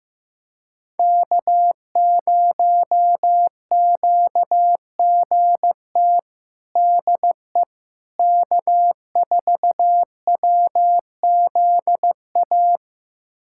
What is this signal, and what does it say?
Morse code, heard as a keyed tone, about 16 wpm, 700 Hz. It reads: K0QGT DE K4WZA
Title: USA Callsign DE Callsign - 5 Characters - 15wpm